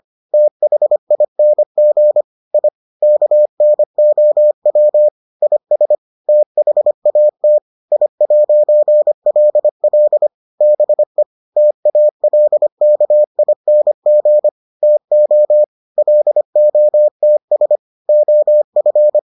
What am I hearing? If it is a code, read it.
Morse code, heard as a keyed tone, about 25 wpm, 600 Hz. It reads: THING I KNOW IS THAT I'LL BE TALKING TO LOTS OF